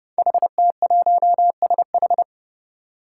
Morse code, heard as a keyed tone, about 30 wpm, 700 Hz. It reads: HT1SH
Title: Five Challenging Letter-Number Combinations 30wpm